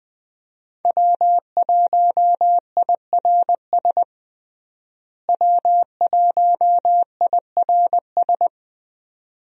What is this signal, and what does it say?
Morse code, heard as a keyed tone, about 20 wpm, 700 Hz. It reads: W1IRS W1IRS
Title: USA Call Signs - 5 Characters - Repeated 2x - 20wpm